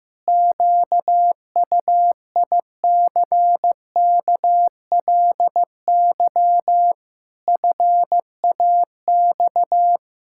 Morse code, heard as a keyed tone, about 15 wpm, 700 Hz. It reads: QUICKLY FAX
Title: Pangram Challenge v2 - 15wpm